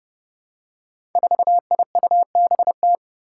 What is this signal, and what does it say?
Morse code, heard as a keyed tone, about 30 wpm, 700 Hz. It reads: 4IUBT
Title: Five Challenging Letter-Number Combinations 30wpm